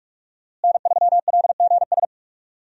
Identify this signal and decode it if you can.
Morse code, heard as a keyed tone, about 46 wpm, 700 Hz. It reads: N3LGS
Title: USA Call Signs - 5 Characters - Repeated 3x - 45wpm